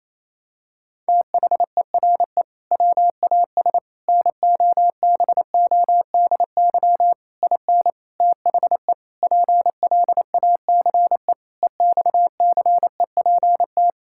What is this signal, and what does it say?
Morse code, heard as a keyed tone, about 28 wpm, 700 Hz. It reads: THERE WAS NOBODY IN THE PLACE EXCEPT
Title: Sentences from Top 1600 Words 28wpm